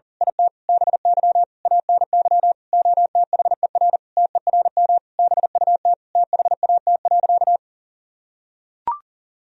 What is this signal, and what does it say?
Morse code, heard as a keyed tone, about 37 wpm, 700 Hz. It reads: IT BY ANY OTHER TERM BUT THAT.
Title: Sentences from Top 1000 Words - Review Part 4 - 40wpm